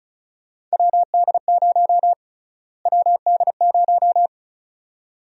Morse code, heard as a keyed tone, about 36 wpm, 700 Hz. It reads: WD0 WD0
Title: USA Call Sign Prefixes - Repeated 2x - 35wpm